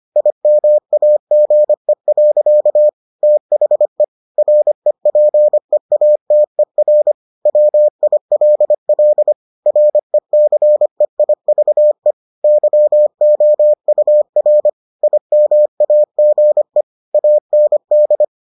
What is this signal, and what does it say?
Morse code, heard as a keyed tone, about 25 wpm, 600 Hz. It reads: IMAGE. THE REPEATER WILL RECEIVE YOUR IMAGE AND